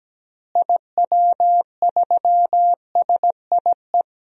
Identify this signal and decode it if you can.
Morse code, heard as a keyed tone, about 17 wpm, 700 Hz. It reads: IW3SIE
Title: DXCC-100 Call Signs - 6 Characters - Repeated 3x - 17wpm